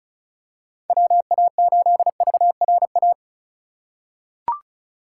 Morse code, heard as a keyed tone, about 34 wpm, 700 Hz. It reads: WA8VRA
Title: USA Call Signs - 6 Characters - 2x Character Spacing - 35wpm